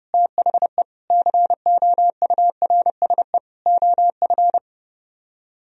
Morse code, heard as a keyed tone, about 30 wpm, 700 Hz. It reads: THE COURSE OF